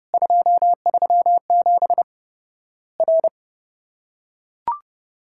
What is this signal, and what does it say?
Morse code, heard as a keyed tone, about 30 wpm, 700 Hz. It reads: 237 R
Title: Worked All Europe DX Contest - QTC Exchange - 30wpm